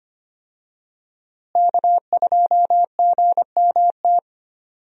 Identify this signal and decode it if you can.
Morse code, heard as a keyed tone, about 25 wpm, 700 Hz. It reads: K2GMT
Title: USA Call Signs - 5 Characters - Repeated 3x - 25wpm